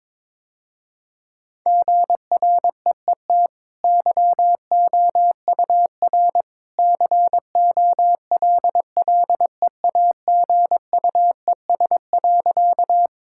Morse code, heard as a keyed tone, about 22 wpm, 700 Hz. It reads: GREET YOUR COLLEAGUES.
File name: practice / 22wpm / Easy Sentences with Hints - 22wpm